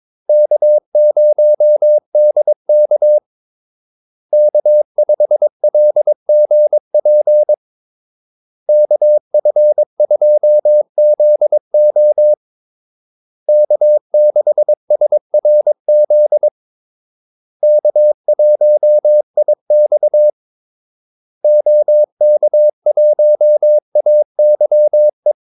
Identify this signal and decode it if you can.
Morse code, heard as a keyed tone, about 22 wpm, 600 Hz. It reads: K0DK K5LGP KF2ZO K6SRZ K1IX OK1AYE
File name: Callsigns-22wpm